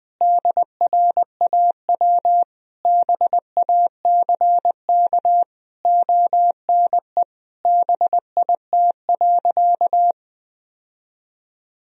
Morse code, heard as a keyed tone, about 20 wpm, 700 Hz. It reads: DRAW BACK ONE BIT.
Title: Sentences from Top 1000 Words - Review Part 1 - 20wpm